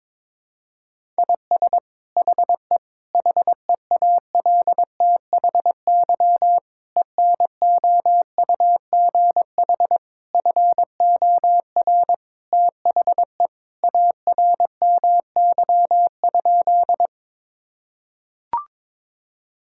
Morse code, heard as a keyed tone, about 22 wpm, 700 Hz. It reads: IS HE HEALTHY ENOUGH FOR THE ARMY?
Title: Sentences from Top 1000 Words - Review Part 5 - 22wpm